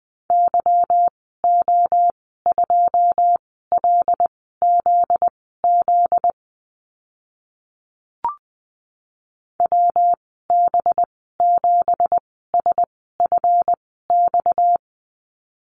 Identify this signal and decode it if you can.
Morse code, heard as a keyed tone, about 20 wpm, spaced as for 14 wpm, 700 Hz. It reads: YO2LZZ WB7SFX
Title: 245 Lesson 39g - Letter z - Call Signs 20wpm